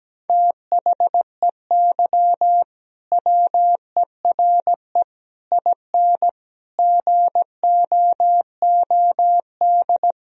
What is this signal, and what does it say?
Morse code, heard as a keyed tone, about 17 wpm, 700 Hz. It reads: THEY WERE IN GOOD
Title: Sentences from Top 1000 Words - Review Part 3 - 17wpm